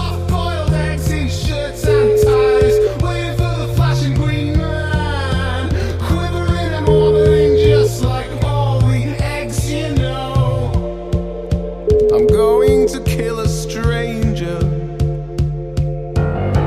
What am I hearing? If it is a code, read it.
Ringback tone (Europe, 425 Hz), with noise.